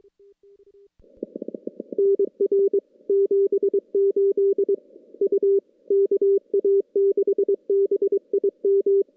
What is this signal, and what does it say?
Morse code, heard as a keyed tone, about 22 wpm, 400 Hz. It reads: NR 78 U KA6BIM
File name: contests / CW-SS-k5tr- -2300